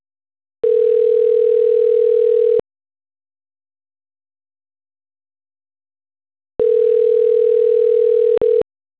Ringback tone (North America), 440 + 480 Hz.